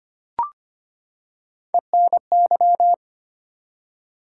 Morse code, heard as a keyed tone, about 25 wpm, 700 Hz. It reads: ENY
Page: morse-code-ninja-practice-25wpm